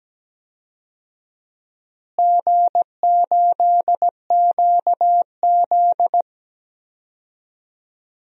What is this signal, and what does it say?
Morse code, heard as a keyed tone, about 17 wpm, 700 Hz. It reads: G8QZ
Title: DXCC-100 Call Signs - 4 Characters - Repeated 2x - 17wpm